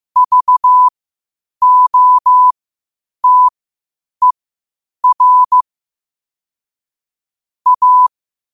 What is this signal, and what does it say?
Morse code, heard as a keyed tone, about 15 wpm, spaced as for 8 wpm, 1000 Hz. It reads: VOTER A
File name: qotdfiles / 8wpm